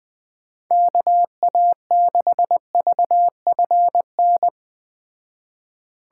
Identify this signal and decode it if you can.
Morse code, heard as a keyed tone, about 19 wpm, 700 Hz. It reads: KA6VFN
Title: Speed Racing - USA 2x3 Call Signs 17wpm